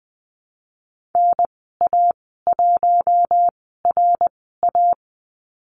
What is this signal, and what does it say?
Morse code, heard as a keyed tone, about 20 wpm, spaced as for 15 wpm, 700 Hz. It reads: NA1RA
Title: 140 Lesson 23g - Letter f - Call Signs 20wpm